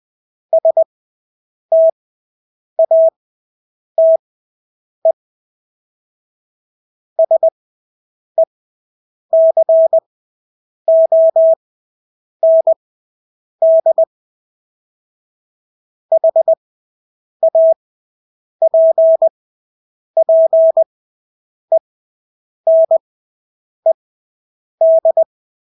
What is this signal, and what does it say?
Morse code, heard as a keyed tone, about 20 wpm, spaced as for 8 wpm, 650 Hz. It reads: STATE SECOND HAPPENED
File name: Engelske ord 20/8 wpm